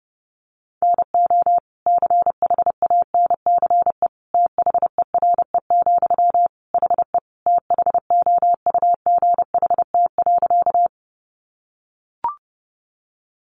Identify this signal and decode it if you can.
Morse code, heard as a keyed tone, about 30 wpm, 700 Hz. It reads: NO CHANCE THERE, HE THOUGHT.
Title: Sentences from Top 800 Words 30wpm